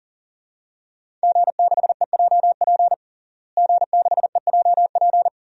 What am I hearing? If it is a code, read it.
Morse code, heard as a keyed tone, about 40 wpm, 700 Hz. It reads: G6EJP G6EJP
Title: DXCC-100 Call Signs - 5 Characters - Repeated 2x - 40wpm